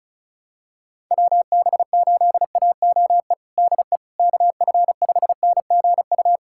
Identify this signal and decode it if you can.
Morse code, heard as a keyed tone, about 36 wpm, 700 Hz. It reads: WB8AOE DE KF5NGU
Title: USA Callsign DE Callsign - 6 Characters - 35wpm